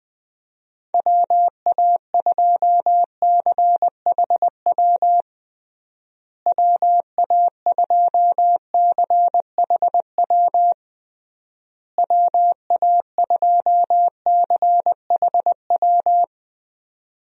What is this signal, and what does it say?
Morse code, heard as a keyed tone, about 19 wpm, 700 Hz. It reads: WA2CHW WA2CHW WA2CHW
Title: USA Call Signs - 6 Characters - Repeated 3x - 20wpm